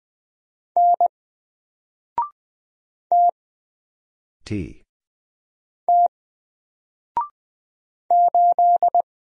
Morse code, heard as a keyed tone, about 20 wpm, 700 Hz. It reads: N T T 8
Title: Single Letter-Number Alt 20wpm